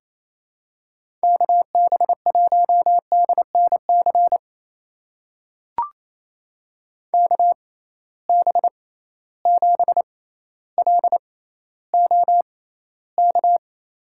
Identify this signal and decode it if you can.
Morse code, heard as a keyed tone, about 28 wpm, spaced as for 14 wpm, 700 Hz. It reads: KB1DNC KB7LOK
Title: USA Call Signs - 6 Characters - 6x Character Spacing - 28wpm